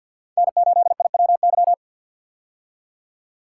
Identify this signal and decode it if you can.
Morse code, heard as a keyed tone, about 53 wpm, 700 Hz. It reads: N8IWY
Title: K1USN SST QSO Practice - 50wpm